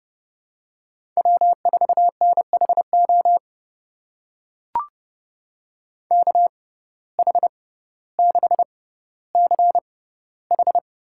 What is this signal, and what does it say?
Morse code, heard as a keyed tone, about 30 wpm, spaced as for 15 wpm, 700 Hz. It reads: W4NHO KH6CH
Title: USA Call Signs - 6x Character Spacing - 30wpm